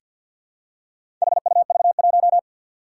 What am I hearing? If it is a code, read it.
Morse code, heard as a keyed tone, about 47 wpm, 700 Hz. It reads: HUV1
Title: Four Challenging Letter-Number Combinations 50wpm